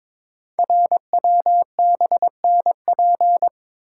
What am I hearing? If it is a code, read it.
Morse code, heard as a keyed tone, about 22 wpm, 700 Hz. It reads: RWBNP
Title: Code Groups - 5 Letters - 22wpm